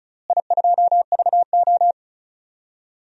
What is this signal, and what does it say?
Morse code, heard as a keyed tone, about 36 wpm, 690 Hz. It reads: I2VO
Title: DXCC-100 Call Signs - 4 Characters - Repeated 2x - 35wpm